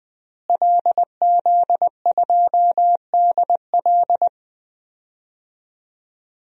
Morse code, heard as a keyed tone, about 20 wpm, 700 Hz. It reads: LZ2DL
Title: DXCC-100 Call Signs - 5 Characters - Repeated 3x - 20wpm